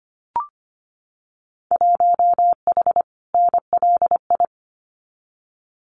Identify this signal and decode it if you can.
Morse code, heard as a keyed tone, about 25 wpm, 700 Hz. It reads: 1H NLI
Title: Winter Field Day Exchanges 25wpm